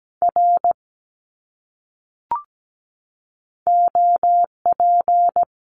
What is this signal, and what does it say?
Morse code, heard as a keyed tone, about 17 wpm, 700 Hz. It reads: R OP
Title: Abbreviations Repeated 3x - 17wpm